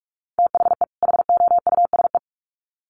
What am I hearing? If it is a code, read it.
Morse code, heard as a keyed tone, about 44 wpm, 700 Hz. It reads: THE HOUSE
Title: Sets of 2 Words 45wpm